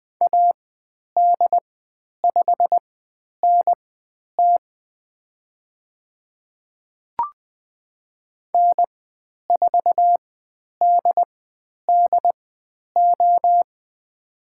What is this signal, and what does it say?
Morse code, heard as a keyed tone, about 20 wpm, spaced as for 10 wpm, 700 Hz. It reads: AD5NT N4DDO
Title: 095 Lesson 16g - Letter c - Call Signs 20wpm